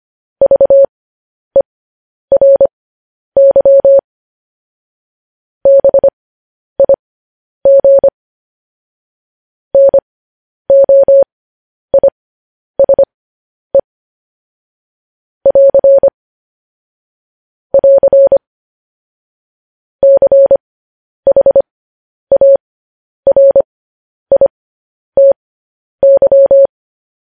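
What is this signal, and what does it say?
Morse code, heard as a keyed tone, about 25 wpm, spaced as for 10 wpm, 550 Hz. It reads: VERY BIG NOISE <AR> <AR> CHARITY
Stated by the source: Morse code news headlines broadcast, opening with vvv.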